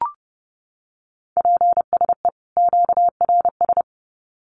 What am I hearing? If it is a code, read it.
Morse code, heard as a keyed tone, about 30 wpm, 700 Hz. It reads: PSE QRS
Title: Common QSO Phrases 30wpm